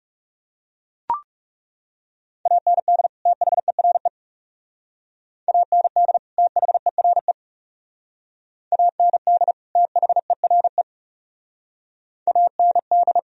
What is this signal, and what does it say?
Morse code, heard as a keyed tone, about 37 wpm, 700 Hz. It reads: AND THERE AND THERE AND THERE AND
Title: Speed-Racing - Sets of 2 Words 30wpm